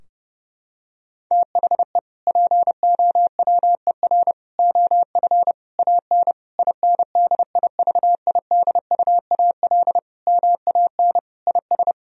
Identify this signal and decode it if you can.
Morse code, heard as a keyed tone, about 30 wpm, 700 Hz. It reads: THE POWER OF AN INDIVIDUAL MAN IS
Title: Sentences from Top 1000 Words - Review Part 4 - 30wpm